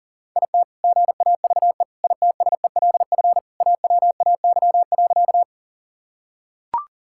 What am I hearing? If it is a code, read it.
Morse code, heard as a keyed tone, about 37 wpm, 700 Hz. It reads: IT GAVE ITSELF AWAY.
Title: Sentences from Top 1000 Words - Review Part 5 - 40wpm